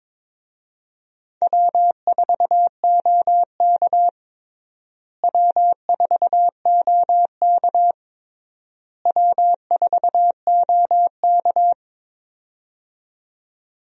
Morse code, heard as a keyed tone, about 22 wpm, 700 Hz. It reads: W4OK W4OK W4OK
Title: USA Call Signs - 4 Characters - Repeated 3x - 22wpm